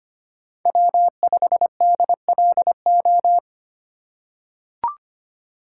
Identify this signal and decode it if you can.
Morse code, heard as a keyed tone, about 25 wpm, 700 Hz. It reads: W5DLO
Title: USA Call Signs - 5 Characters - 2x Character Spacing - 25wpm